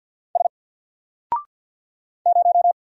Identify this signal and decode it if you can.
Morse code, heard as a keyed tone, about 50 wpm, 700 Hz. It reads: S 0
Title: Single Character (Letter:Number:Punctuation:Prosign) 50wpm